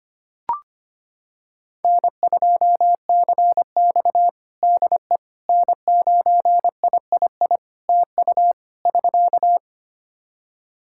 Morse code, heard as a keyed tone, about 25 wpm, 700 Hz. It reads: N2CX DE N9III TU <SK>
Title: K1USN SST QSO Practice 25wpm